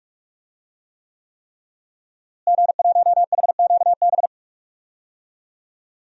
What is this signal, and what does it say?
Morse code, heard as a keyed tone, about 46 wpm, 700 Hz. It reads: G1HQB
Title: DXCC-100 Call Signs - 5 Characters - Repeated 3x - 45wpm